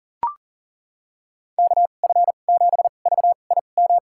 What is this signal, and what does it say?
Morse code, heard as a keyed tone, about 40 wpm, spaced as for 29 wpm, 700 Hz. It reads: KF7VIM
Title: USA Call Signs - 6 Characters - 2x Character Spacing - 40wpm